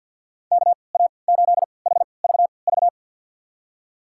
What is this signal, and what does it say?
Morse code, heard as a keyed tone, about 52 wpm, spaced as for 29 wpm, 700 Hz. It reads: KA8HVV
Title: USA Call Signs - 3x Character Spacing - 50wpm